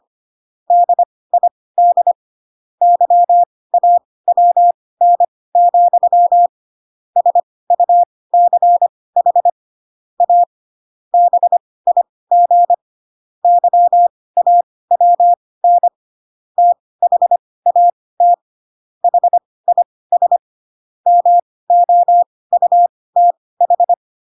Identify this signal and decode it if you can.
Morse code, heard as a keyed tone, about 25 wpm, spaced as for 18 wpm, 700 Hz. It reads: DID YAWN, SUCH A BIG YAWN THAT HIS MOUTH
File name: Pinocchio / Pinocchio ch5 12 25 ik1hge